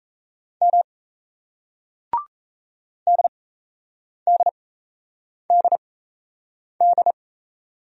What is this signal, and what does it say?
Morse code, heard as a keyed tone, about 38 wpm, spaced as for 8 wpm, 700 Hz. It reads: M DDDD